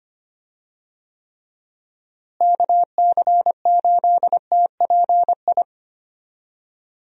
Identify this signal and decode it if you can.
Morse code, heard as a keyed tone, about 25 wpm, 700 Hz. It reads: KC8TPI
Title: USA Call Signs - 6 Characters - Repeated 2x - 25wpm